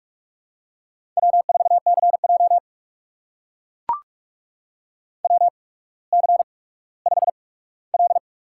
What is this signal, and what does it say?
Morse code, heard as a keyed tone, about 44 wpm, spaced as for 19 wpm, 690 Hz. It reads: W4CJ WC5L
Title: USA Call Signs - 4 Characters - 8x Character Spacing - 45wpm